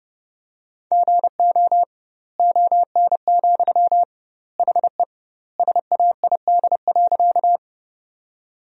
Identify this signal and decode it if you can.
Morse code, heard as a keyed tone, about 32 wpm, 700 Hz. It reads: GO ON, HE SAID.
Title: 2x Word-Spacing - Sentences from Top 100 Words - 30wpm